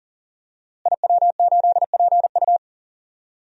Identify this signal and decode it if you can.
Morse code, heard as a keyed tone, about 40 wpm, 700 Hz. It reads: IW8PU
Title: DXCC-100 Call Signs - 5 Characters - Repeated 2x - 40wpm